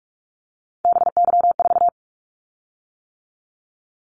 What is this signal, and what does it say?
Morse code, heard as a keyed tone, about 46 wpm, 700 Hz. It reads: BY4
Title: Callsign Prefixes for Top 100 DXCC - 45wpm